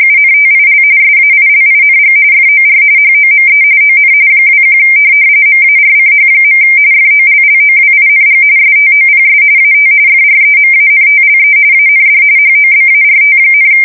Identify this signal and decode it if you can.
RTTY, about 45.45 baud. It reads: UNTIL SAILING FOR HOME ON 21 MARCH. FOLLOWING STOPS AT FREMANTLE AND HOBART,
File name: fsk170 / USS-Missouri